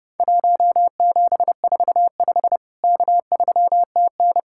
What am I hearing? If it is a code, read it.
Morse code, heard as a keyed tone, about 30 wpm, 700 Hz. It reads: 1745 K3TN
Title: Worked All Europe DX Contest - QTC Exchange - 30wpm